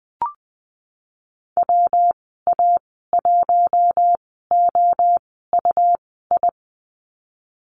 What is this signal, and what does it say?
Morse code, heard as a keyed tone, about 20 wpm, spaced as for 15 wpm, 700 Hz. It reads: WA1OUI
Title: 167 Lesson 27g - Number 7 - Call Signs 20wpm